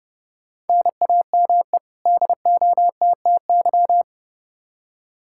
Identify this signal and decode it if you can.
Morse code, heard as a keyed tone, about 30 wpm, 700 Hz. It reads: NAME DOTTY
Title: K1USN SST QSO Practice 30wpm